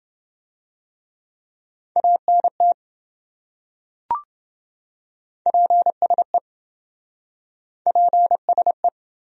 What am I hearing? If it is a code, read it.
Morse code, heard as a keyed tone, about 26 wpm, 700 Hz. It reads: ANT PSE PSE